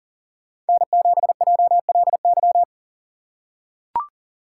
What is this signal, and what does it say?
Morse code, heard as a keyed tone, about 40 wpm, 700 Hz. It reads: N7JLY
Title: USA Call Signs - 5 Characters - 3x Character Spacing - 40wpm